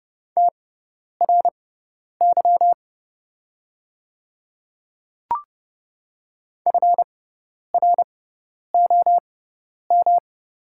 Morse code, heard as a keyed tone, about 30 wpm, spaced as for 10 wpm, 690 Hz. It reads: TRY FROM
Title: 6x Character Spacing - Top 500 Words - 30wpm